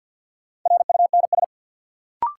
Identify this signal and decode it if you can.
Morse code, heard as a keyed tone, about 52 wpm, 700 Hz. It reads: RUNS
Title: Sentences from Top 600 Words - New Word Drill - 50wpm